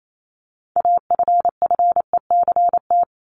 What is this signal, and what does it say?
Morse code, heard as a keyed tone, about 28 wpm, 700 Hz. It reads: AFFECT
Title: Top 751-1000 Words 28wpm